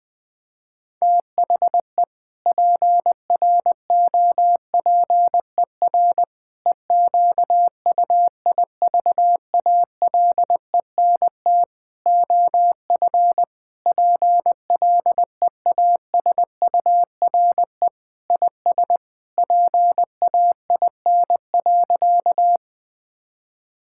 Morse code, heard as a keyed tone, about 20 wpm, 700 Hz. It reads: THE PROPER EQUIVALENT OF PLEASURE IS PAIN.
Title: Sentences from Top 1600 Words 20wpm